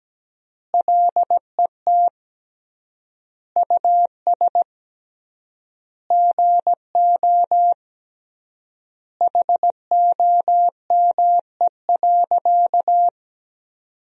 Morse code, heard as a keyed tone, about 17 wpm, 700 Hz. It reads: LET US GO HOME.
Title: 3x Word-Spacing - Sentences from Top 300 Words - 17wpm